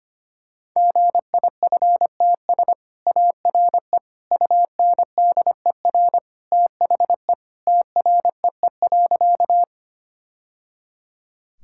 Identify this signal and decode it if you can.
Morse code, heard as a keyed tone, about 25 wpm, 700 Hz. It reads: GIFTS ARE UNDER THE TREE.
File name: Easy Sentences with Hints - 25wpm